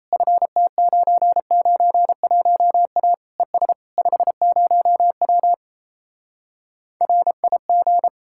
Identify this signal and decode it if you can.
Morse code, heard as a keyed tone, about 37 wpm, 700 Hz. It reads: FT991A ES 50W RIG